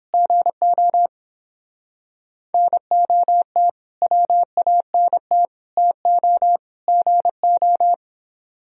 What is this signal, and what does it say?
Morse code, heard as a keyed tone, about 29 wpm, 700 Hz. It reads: GO NOT WANT TO GO